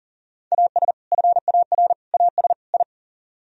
Morse code, heard as a keyed tone, about 49 wpm, 700 Hz. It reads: AS FAR AS I
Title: Speed-Racing - Sets of 4 Words 40wpm